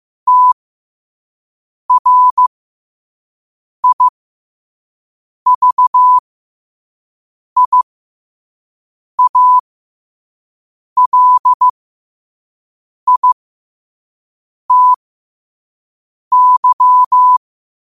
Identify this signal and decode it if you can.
Morse code, heard as a keyed tone, about 15 wpm, spaced as for 5 wpm, 1000 Hz. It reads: TRIVIALITY
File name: qotdfiles / 5wpm